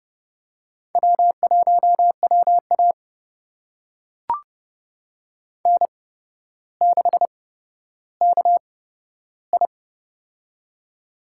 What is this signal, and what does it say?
Morse code, heard as a keyed tone, about 30 wpm, spaced as for 13 wpm, 700 Hz. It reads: W1WA N6KI
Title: USA Call Signs - 4 Characters - 8x Character Spacing - 30wpm